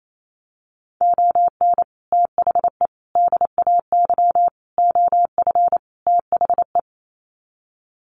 Morse code, heard as a keyed tone, about 28 wpm, 700 Hz. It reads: ON THE DAY OF THE
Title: Sets of 5 Words Encore 28wpm